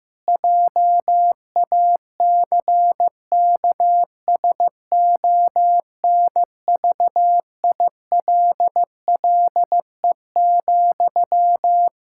Morse code, heard as a keyed tone, about 16 wpm, 700 Hz. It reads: JACKSONVILLE,
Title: Practice QSO - Exchange 1 - Beginning - 15wpm